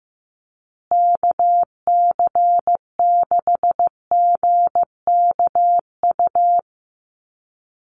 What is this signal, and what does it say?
Morse code, heard as a keyed tone, about 15 wpm, 700 Hz. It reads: KC6GKU
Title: USA Call Signs Repeated 3x 15wpm